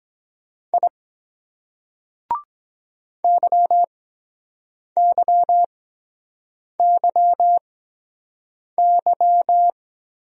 Morse code, heard as a keyed tone, about 23 wpm, spaced as for 7 wpm, 700 Hz. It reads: I YYYY